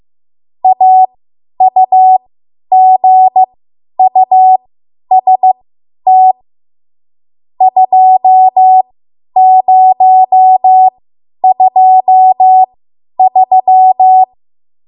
Morse code, heard as a keyed tone, about 15 wpm, spaced as for 10 wpm, 750 Hz. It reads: AUGUST 2023
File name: Morse / 10WPM